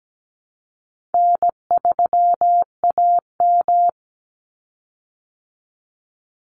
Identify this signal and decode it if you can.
Morse code, heard as a keyed tone, about 17 wpm, 700 Hz. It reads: N3AM
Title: Easy Call Signs Repeated 3x - 17wpm